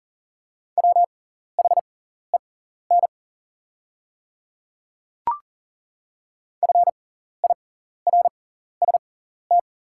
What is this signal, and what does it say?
Morse code, heard as a keyed tone, about 38 wpm, spaced as for 14 wpm, 700 Hz. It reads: WHEN FIRST
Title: 6x Character Spacing - Top 300 Words - 40wpm